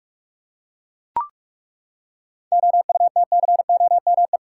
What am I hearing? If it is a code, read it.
Morse code, heard as a keyed tone, about 43 wpm, 700 Hz. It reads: OUTCOME
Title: Seven Letter Words - 45wpm